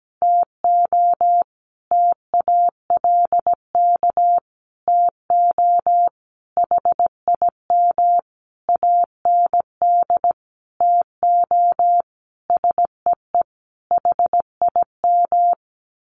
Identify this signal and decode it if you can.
Morse code, heard as a keyed tone, about 17 wpm, 700 Hz. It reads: TO TALK TO HIM AND TO SEE HIM
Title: Sentences from Top 1400 Words 17wpm